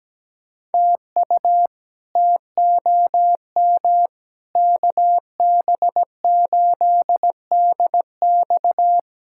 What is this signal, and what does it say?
Morse code, heard as a keyed tone, about 17 wpm, 700 Hz. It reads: TU TOM KB8DX
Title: K1USN SST QSO Practice - 17wpm